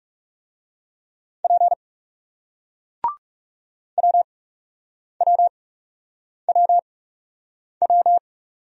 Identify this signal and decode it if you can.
Morse code, heard as a keyed tone, about 42 wpm, spaced as for 8 wpm, 700 Hz. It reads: P WWWW